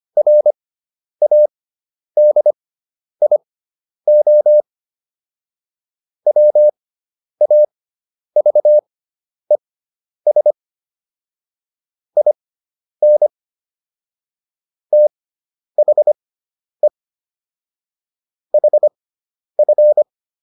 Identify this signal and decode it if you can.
Morse code, heard as a keyed tone, about 25 wpm, spaced as for 10 wpm, 600 Hz. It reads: RADIO WAVES IN THE HF